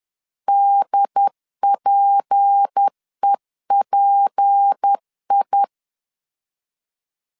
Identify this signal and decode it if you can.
Morse code, heard as a keyed tone, about 10 wpm, 800 Hz. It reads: DPEPI
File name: LEZIONI / 6 PKX0 / 6 P 10wpm 2432